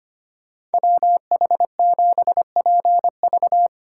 Morse code, heard as a keyed tone, about 32 wpm, 700 Hz. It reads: WH7PV